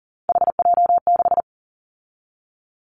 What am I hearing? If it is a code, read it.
Morse code, heard as a keyed tone, about 40 wpm, 700 Hz. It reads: HJ6
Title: Callsign Prefixes for Top 100 DXCC - 40wpm